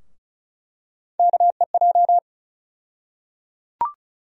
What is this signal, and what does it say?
Morse code, heard as a keyed tone, about 34 wpm, 700 Hz. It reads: KEJ
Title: USA Call Sign Postfixes - 3 Letters 35wpm